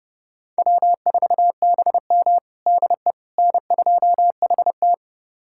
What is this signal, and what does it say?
Morse code, heard as a keyed tone, about 30 wpm, 700 Hz. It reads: W4BM DE N2HT
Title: USA Callsign DE Callsign - 4 Characters - 30wpm